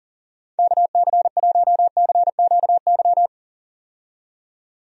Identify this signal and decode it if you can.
Morse code, heard as a keyed tone, about 40 wpm, 700 Hz. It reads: KC1CQY
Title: USA Call Signs - 6 Characters - 40wpm